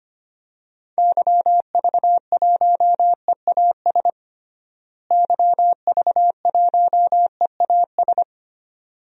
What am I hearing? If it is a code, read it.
Morse code, heard as a keyed tone, about 25 wpm, 700 Hz. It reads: YV1EAS YV1EAS
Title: DXCC-100 Call Signs - 6 Characters - Repeated 2x - 25wpm